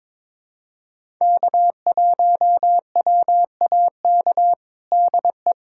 Morse code, heard as a keyed tone, about 22 wpm, 700 Hz. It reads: K1WAK DE
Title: USA Callsign DE Callsign - 5 Characters - 22wpm